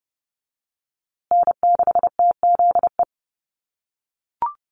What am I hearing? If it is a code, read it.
Morse code, heard as a keyed tone, about 30 wpm, 700 Hz. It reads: N6TZE
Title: USA Call Signs Repeated 2x 30wpm